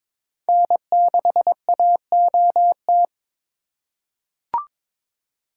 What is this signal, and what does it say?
Morse code, heard as a keyed tone, about 22 wpm, 700 Hz. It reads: N6AOT
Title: USA Call Signs - 5 Characters - 3x Character Spacing - 22wpm